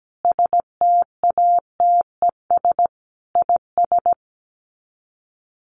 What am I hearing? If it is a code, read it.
Morse code, heard as a keyed tone, about 17 wpm, 700 Hz. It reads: STATES IS
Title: Sets of 4 Words 17wpm